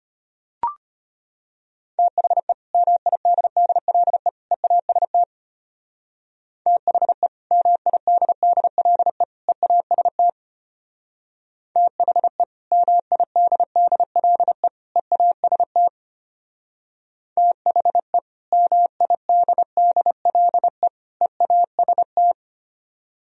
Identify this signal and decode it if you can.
Morse code, heard as a keyed tone, about 30 wpm, 700 Hz. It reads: THE MIDDLE EAST THE MIDDLE EAST THE MIDDLE EAST THE MIDDLE EAST
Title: Speed-Racing - Sets of 3 Words 25wpm